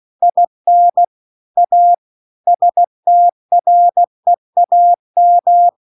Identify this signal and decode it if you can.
Morse code, heard as a keyed tone, about 16 wpm, 700 Hz. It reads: IN A STREAM